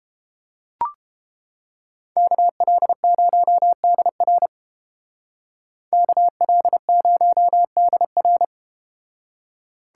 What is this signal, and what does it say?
Morse code, heard as a keyed tone, about 31 wpm, 700 Hz. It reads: KL0DR KL0DR